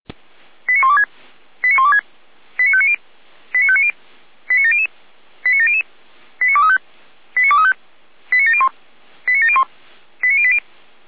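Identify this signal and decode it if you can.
Five-tone selective call (ZVEI), heard as quick runs of five tones, tones of about 70 ms. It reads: 89126 89126 89590 89590 89700 89700 89235 89235 89701 89701 89080
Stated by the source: ZVEI five-tone selective call received on SDR.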